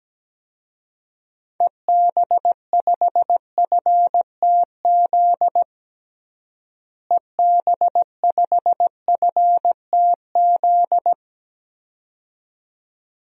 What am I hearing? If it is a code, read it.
Morse code, heard as a keyed tone, about 17 wpm, 700 Hz. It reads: EB5FTZ EB5FTZ
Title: DXCC-100 Call Signs - 6 Characters - Repeated 2x - 17wpm